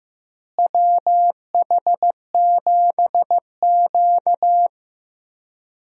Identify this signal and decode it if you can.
Morse code, heard as a keyed tone, about 15 wpm, 700 Hz. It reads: WH7Q
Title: USA Call Signs - 4 Characters - Repeated 3x - 15wpm